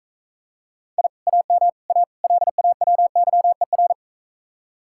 Morse code, heard as a keyed tone, about 44 wpm, 700 Hz. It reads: I AM A LAWYER